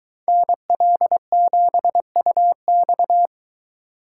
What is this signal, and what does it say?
Morse code, heard as a keyed tone, about 22 wpm, 700 Hz. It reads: NL7UX